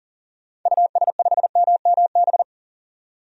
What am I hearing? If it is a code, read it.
Morse code, heard as a keyed tone, about 40 wpm, 700 Hz. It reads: US5MMB
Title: DXCC-100 Call Signs - 6 Characters - 40wpm